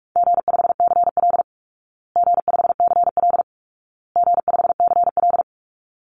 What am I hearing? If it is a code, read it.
Morse code, heard as a keyed tone, about 44 wpm, 700 Hz. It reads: G5CL G5CL G5CL
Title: Easy Call Signs Repeated 3x - 45wpm